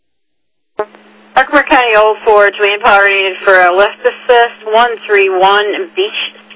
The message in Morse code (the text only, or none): none